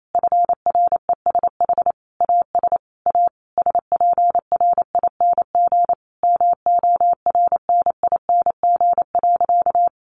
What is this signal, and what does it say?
Morse code, heard as a keyed tone, about 28 wpm, 700 Hz. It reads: FRESH AS A SPRING MORNING.
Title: Sentences from Top 1200 Words 28wpm